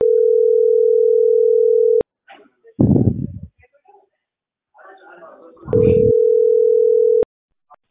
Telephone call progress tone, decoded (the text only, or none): ringback tone